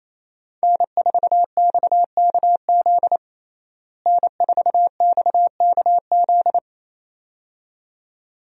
Morse code, heard as a keyed tone, about 28 wpm, 700 Hz. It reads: N4XKZ N4XKZ
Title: USA Call Signs - 5 Characters - Repeated 2x - 28wpm